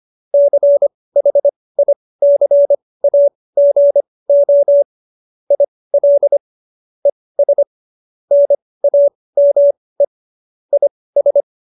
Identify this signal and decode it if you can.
Morse code, heard as a keyed tone, about 25 wpm, spaced as for 18 wpm, 570 Hz. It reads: CHICAGO IL ES NAME IS
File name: qso222 18